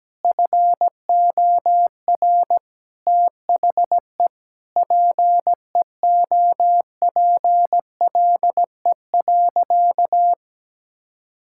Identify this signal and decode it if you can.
Morse code, heard as a keyed tone, about 17 wpm, 700 Hz. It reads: FOR THE PEOPLE.
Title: Sentences from Top 1000 Words - Review Part 5 - 17wpm